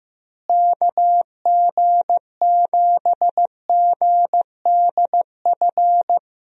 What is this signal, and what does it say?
Morse code, heard as a keyed tone, about 15 wpm, 700 Hz. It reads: KG7GDF
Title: USA Call Signs - 6 Characters - Repeated 3x - 15wpm